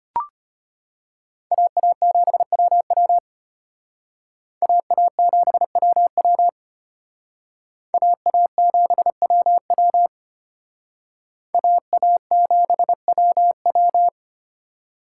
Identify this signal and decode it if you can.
Morse code, heard as a keyed tone, about 32 wpm, 700 Hz. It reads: AA7WW AA7WW AA7WW AA7WW